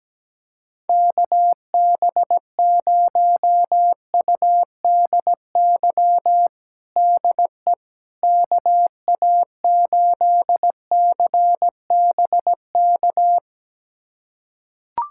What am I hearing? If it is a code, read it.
Morse code, heard as a keyed tone, about 17 wpm, 700 Hz. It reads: KB0UDY DE KA8CBK
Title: USA Callsign DE Callsign - 6 Characters - 17wpm